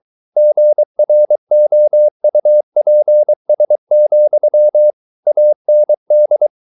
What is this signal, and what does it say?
Morse code, heard as a keyed tone, about 23 wpm, 600 Hz. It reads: GROUPS, AND